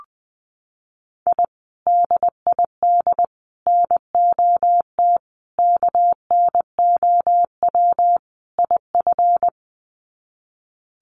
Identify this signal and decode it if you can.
Morse code, heard as a keyed tone, about 20 wpm, 700 Hz. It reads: I DID NOT KNOW IF
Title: Sets of 5 Words Encore 20wpm